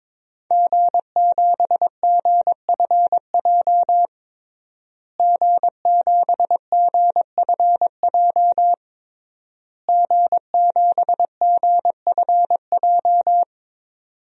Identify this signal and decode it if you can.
Morse code, heard as a keyed tone, about 22 wpm, 700 Hz. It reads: G7GFJ G7GFJ G7GFJ
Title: DXCC-100 Call Signs - 5 Characters - Repeated 3x - 22wpm